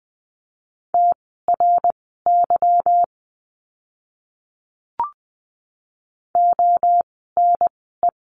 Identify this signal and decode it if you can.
Morse code, heard as a keyed tone, about 20 wpm, spaced as for 15 wpm, 700 Hz. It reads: TRY ONE
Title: 160 Lesson 26f - Letter g - Words 20wpm